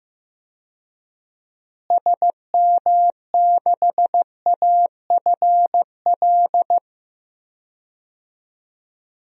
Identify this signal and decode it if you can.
Morse code, heard as a keyed tone, about 16 wpm, 700 Hz. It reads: SM6AFL
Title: DXCC-100 Call Signs - 6 Characters - Repeated 3x - 15wpm